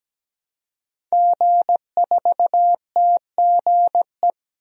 Morse code, heard as a keyed tone, about 17 wpm, 700 Hz. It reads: G4TGE
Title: DXCC-100 Call Signs - 5 Characters - Repeated 3x - 17wpm